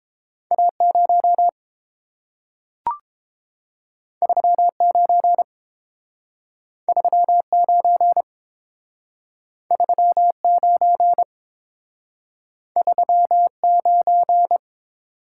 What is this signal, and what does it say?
Morse code, heard as a keyed tone, about 30 wpm, 700 Hz. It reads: A0 39 39 39 39